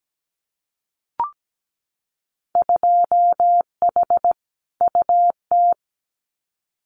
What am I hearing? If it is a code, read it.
Morse code, heard as a keyed tone, about 17 wpm, 700 Hz. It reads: 2H UT
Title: Winter Field Day Exchanges Encore 17wpm